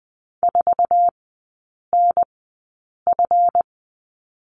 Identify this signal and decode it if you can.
Morse code, heard as a keyed tone, about 20 wpm, spaced as for 8 wpm, 700 Hz. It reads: 4NF
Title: 166 Lesson 27e - Number 7 - Three Characters 20wpm